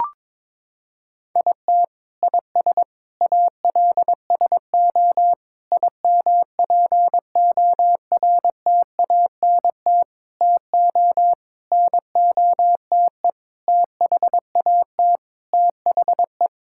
Morse code, heard as a keyed tone, about 22 wpm, 700 Hz. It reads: IT IS ALSO IMPORTANT TO NOTE THAT THE
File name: practice / 22wpm / Sentences with 8-Word Phrases - 22wpm